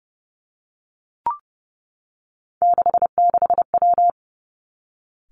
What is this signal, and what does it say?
Morse code, heard as a keyed tone, about 30 wpm, 700 Hz. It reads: 66W
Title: Three Challenging Letter-Number Combinations 30wpm